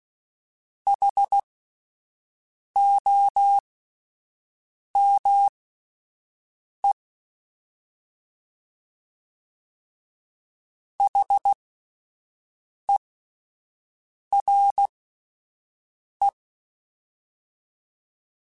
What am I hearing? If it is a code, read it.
Morse code, heard as a keyed tone, about 16 wpm, spaced as for 5 wpm, 800 Hz. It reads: HOME HERE
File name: Alphabetical / Alphabetical mp3s / Hs @5